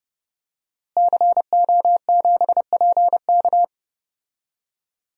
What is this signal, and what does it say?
Morse code, heard as a keyed tone, about 30 wpm, 700 Hz. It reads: CO7PK
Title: DXCC-100 Call Signs - 5 Characters - Repeated 2x - 30wpm